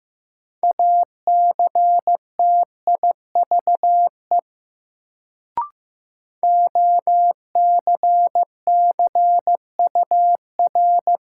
Morse code, heard as a keyed tone, about 15 wpm, 700 Hz. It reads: ACTIVE OCCUR
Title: Sentences from Top 900 Words - New Word Drill - 15wpm